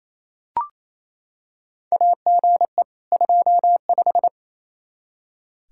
Morse code, heard as a keyed tone, about 28 wpm, 700 Hz. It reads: AGE 25
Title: Practice QSO - Exchange 3 - More Info - 28wpm